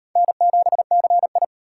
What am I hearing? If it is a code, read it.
Morse code, heard as a keyed tone, about 34 wpm, 700 Hz. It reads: N7CI